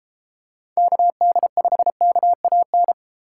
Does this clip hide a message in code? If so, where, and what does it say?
Yes: Morse code from 0.8 to 2.9 s, heard as a keyed tone, about 31 wpm, 700 Hz. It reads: KD5KAN